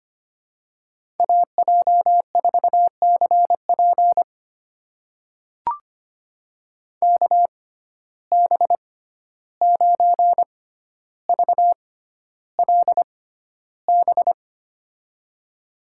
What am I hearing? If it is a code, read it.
Morse code, heard as a keyed tone, about 25 wpm, spaced as for 12 wpm, 700 Hz. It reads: AJ4CP KB9VLB
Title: USA Call Signs - 6x Character Spacing - 25wpm